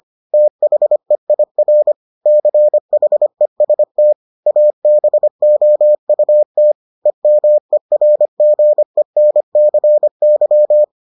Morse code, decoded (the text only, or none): THEIR CHEST ABOUT EMERGENCY